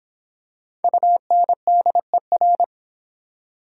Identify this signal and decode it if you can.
Morse code, heard as a keyed tone, about 25 wpm, 700 Hz. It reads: UNDER